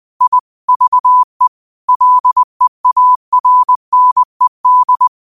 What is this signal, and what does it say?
Morse code, heard as a keyed tone, about 20 wpm, 1000 Hz. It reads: I VE LEARNED
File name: qotdfiles / 20wpm